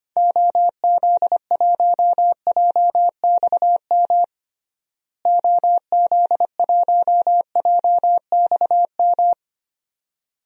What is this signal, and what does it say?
Morse code, heard as a keyed tone, about 25 wpm, 700 Hz. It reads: OZ1JXM OZ1JXM
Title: DXCC-100 Call Signs - 6 Characters - Repeated 2x - 25wpm